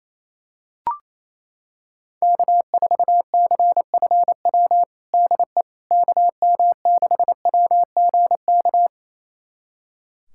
Morse code, heard as a keyed tone, about 28 wpm, 700 Hz. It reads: K4CFW DE KM6WGK
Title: USA Callsign DE Callsign 28wpm